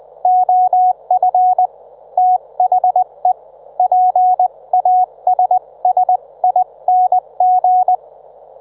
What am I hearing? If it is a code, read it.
Morse code, heard as a keyed tone, about 20 wpm, 730 Hz. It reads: OF THE PASSING
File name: Trivia